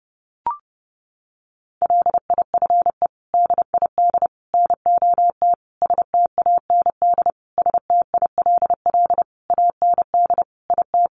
Morse code, heard as a keyed tone, about 28 wpm, 700 Hz. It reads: LIFE DID NOT STAND STILL AND IT
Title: Sentences from Top 800 Words 30wpm